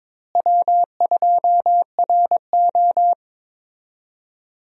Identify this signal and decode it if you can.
Morse code, heard as a keyed tone, about 22 wpm, 700 Hz. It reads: W2RO
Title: USA Call Signs - 4 Characters - Repeated 3x - 22wpm